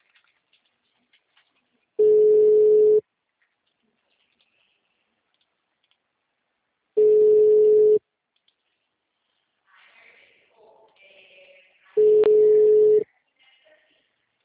Ringback tone (Europe, 425 Hz).